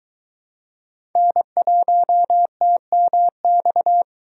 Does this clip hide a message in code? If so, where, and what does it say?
Yes: Morse code from 1.2 to 4.0 s, heard as a keyed tone, about 22 wpm, 700 Hz. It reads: N1TMX